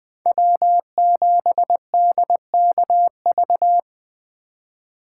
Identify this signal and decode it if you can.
Morse code, heard as a keyed tone, about 24 wpm, 700 Hz. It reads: W7DKV
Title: Speed Racing - USA 1x3 Call Signs 20wpm